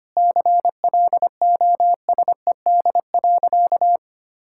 Morse code, heard as a keyed tone, about 25 wpm, 700 Hz. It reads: CLOSED.
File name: Easy Sentences with Hints - 25wpm